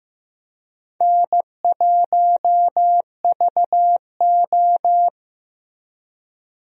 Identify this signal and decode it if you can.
Morse code, heard as a keyed tone, about 15 wpm, 700 Hz. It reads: N1VO
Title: USA Call Signs - 4 Characters - Repeated 2x - 15wpm